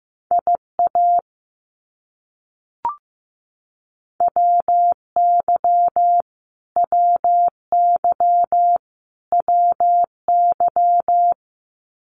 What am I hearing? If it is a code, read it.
Morse code, heard as a keyed tone, about 15 wpm, 700 Hz. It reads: IA WY WY WY
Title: US States Repeated 3x 15wpm